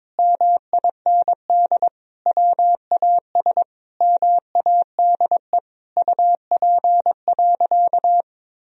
Morse code, heard as a keyed tone, about 22 wpm, 700 Hz. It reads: MIND WAS MADE UP.
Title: Sentences from Top 1000 Words - Review Part 2 - 22wpm